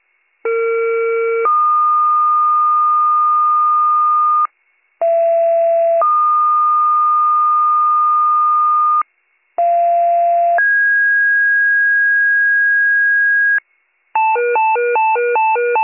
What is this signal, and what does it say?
Two-tone page, heard as a one-second tone, then a three-second tone, about 460 then 1220 Hz, about 670 then 1220 Hz, about 690 then 1650 Hz. It is heard from 0.4 to 13.6 s.